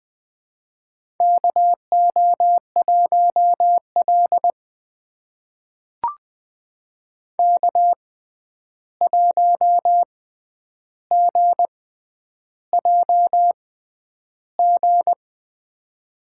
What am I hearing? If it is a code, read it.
Morse code, heard as a keyed tone, about 20 wpm, spaced as for 10 wpm, 700 Hz. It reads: KO1L K1GJG
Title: USA Call Signs - 6x Character Spacing - 20wpm